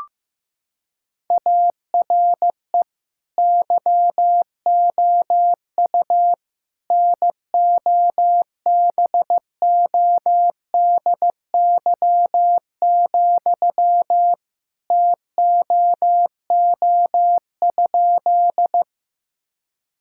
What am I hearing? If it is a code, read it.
Morse code, heard as a keyed tone, about 15 wpm, 700 Hz. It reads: ARE YOU NOBODY, TOO?
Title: Sentences from Top 1600 Words 15wpm